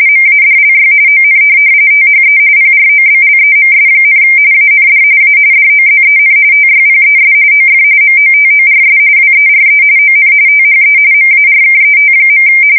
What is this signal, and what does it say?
RTTY, about 45.45 baud. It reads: MADLY ON THE PROBLEMS, AND A DOZEN TYPES OF MOTORS WERE ON THE MARKET.